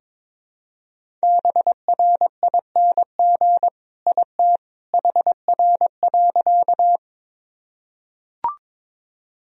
Morse code, heard as a keyed tone, about 21 wpm, 700 Hz. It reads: BRING IT HR.
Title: Sentences from Top 500 Words and Abbreviations 22wpm